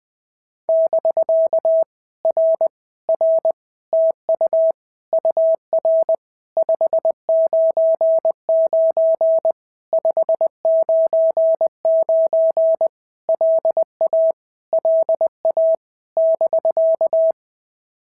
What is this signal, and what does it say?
Morse code, heard as a keyed tone, about 19 wpm, 650 Hz. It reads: <BK> R R TU UR 599 599 LA LA <BK>
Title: POTA QSO Practice - 20wpm